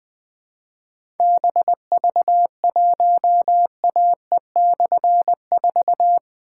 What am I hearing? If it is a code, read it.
Morse code, heard as a keyed tone, about 20 wpm, 700 Hz. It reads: BV1AE/4
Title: Difficult Call Signs 20wpm